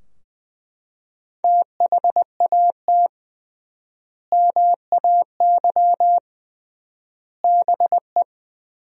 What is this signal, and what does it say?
Morse code, heard as a keyed tone, about 20 wpm, 700 Hz. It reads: THAT MAY BE
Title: 3x Word-Spacing - Sentences from Top 400 Words - 20wpm